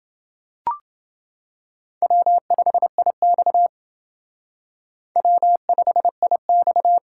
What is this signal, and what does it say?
Morse code, heard as a keyed tone, about 28 wpm, 700 Hz. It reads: W5IX W5IX